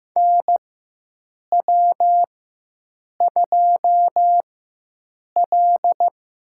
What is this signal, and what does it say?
Morse code, heard as a keyed tone, about 15 wpm, spaced as for 7 wpm, 700 Hz. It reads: NW2L
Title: USA Call Signs - 4 Characters - 4x Character Spacing - 15wpm